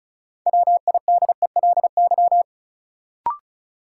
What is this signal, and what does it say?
Morse code, heard as a keyed tone, about 34 wpm, 700 Hz. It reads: WIDELY
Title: Sentences from Top 1200 Words - New Word Drill Alt - 35wpm